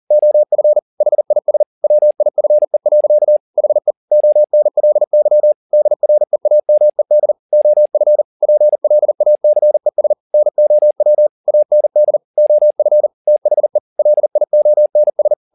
Morse code, heard as a keyed tone, about 40 wpm, 600 Hz. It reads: OF HIS WIFE. HE ONLY DREAMED OF PLACES NOW AND OF THE LIONS